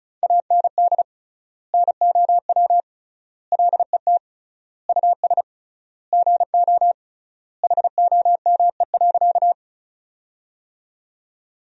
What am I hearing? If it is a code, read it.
Morse code, heard as a keyed tone, about 36 wpm, 700 Hz. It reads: AND NOW LET US GO HOME.
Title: 3x Word-Spacing - Sentences from Top 300 Words - 35wpm